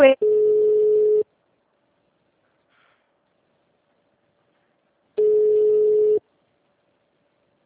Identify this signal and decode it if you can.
Ringback tone (Europe, 425 Hz), with speech.